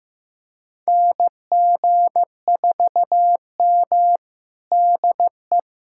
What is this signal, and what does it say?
Morse code, heard as a keyed tone, about 15 wpm, 700 Hz. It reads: NG4M DE
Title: USA Callsign DE Callsign - 4 Characters - 15wpm